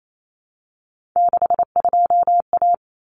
Morse code, heard as a keyed tone, about 28 wpm, 700 Hz. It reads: 62A
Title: Common Cut Numbers - 3-Digit 28wpm